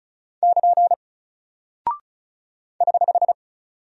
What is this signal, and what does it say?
Morse code, heard as a keyed tone, about 36 wpm, 700 Hz. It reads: <KN> <HH>
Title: Single Character (Letter:Number:Punctuation:Prosign) Alt 35wpm